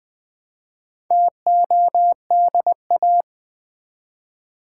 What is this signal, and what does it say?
Morse code, heard as a keyed tone, about 20 wpm, 700 Hz. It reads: TODA
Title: Build A Word - 1 Letter At A Time - 3 to 5-Letter Words - 20wpm